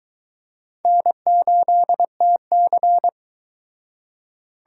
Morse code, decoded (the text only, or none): N8TC